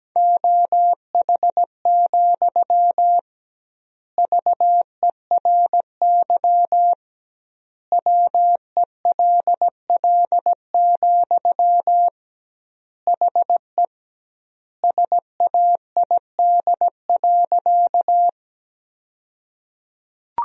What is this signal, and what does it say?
Morse code, heard as a keyed tone, about 17 wpm, 700 Hz. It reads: OH, VERY WELL, HE SAID.
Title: 2x Word-Spacing - Sentences from Top 500 Words - 17wpm